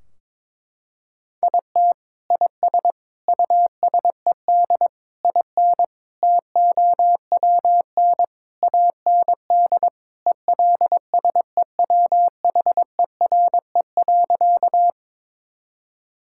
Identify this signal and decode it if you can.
Morse code, heard as a keyed tone, about 22 wpm, 700 Hz. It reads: IT IS USED IN TOWN AND ELSEWHERE.
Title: Sentences from Top 1600 Words 22wpm